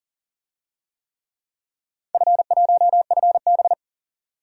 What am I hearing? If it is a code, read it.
Morse code, heard as a keyed tone, about 40 wpm, 700 Hz. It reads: F1FB
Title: DXCC-100 Call Signs - 4 Characters - 40wpm